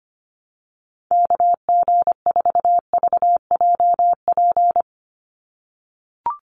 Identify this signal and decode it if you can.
Morse code, heard as a keyed tone, about 25 wpm, 700 Hz. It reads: KG4VJP
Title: USA Call Signs 25wpm